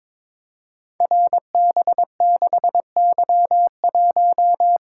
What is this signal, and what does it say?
Morse code, heard as a keyed tone, about 22 wpm, 700 Hz. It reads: RB6Y1
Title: Code Groups - 5 Letter:Numbers - 22wpm